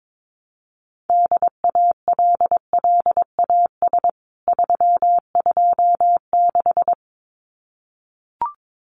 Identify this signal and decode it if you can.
Morse code, heard as a keyed tone, about 22 wpm, 700 Hz. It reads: DALLAS 326
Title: CWT Exchange 22wpm